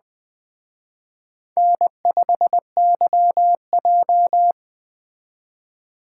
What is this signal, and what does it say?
Morse code, heard as a keyed tone, about 21 wpm, 700 Hz. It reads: N5YJ